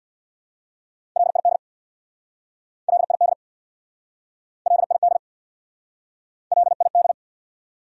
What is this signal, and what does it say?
Morse code, heard as a keyed tone, about 69 wpm, 700 Hz. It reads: LID LID LID LID